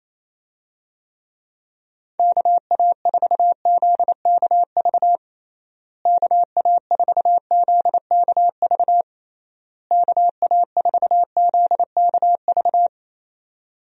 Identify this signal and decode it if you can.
Morse code, heard as a keyed tone, about 28 wpm, 700 Hz. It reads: KA4ZKV KA4ZKV KA4ZKV
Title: USA Call Signs - 6 Characters - Repeated 3x - 28wpm